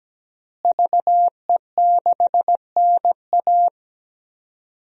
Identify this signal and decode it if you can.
Morse code, heard as a keyed tone, about 17 wpm, 700 Hz. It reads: VE6NA
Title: DXCC-100 Call Signs - 5 Characters - Repeated 2x - 17wpm